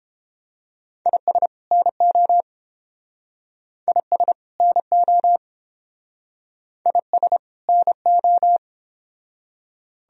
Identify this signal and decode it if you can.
Morse code, heard as a keyed tone, about 30 wpm, 700 Hz. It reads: IS NO IS NO IS NO